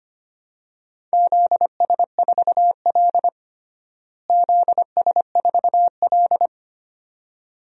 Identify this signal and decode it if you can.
Morse code, heard as a keyed tone, about 25 wpm, 700 Hz. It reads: ZS4L ZS4L
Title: DXCC-100 Call Signs - 4 Characters - Repeated 2x - 25wpm